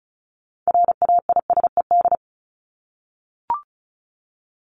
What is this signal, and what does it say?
Morse code, heard as a keyed tone, about 34 wpm, 700 Hz. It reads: RAISED
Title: ED Suffix - Top 100 Words with Base Words 35wpm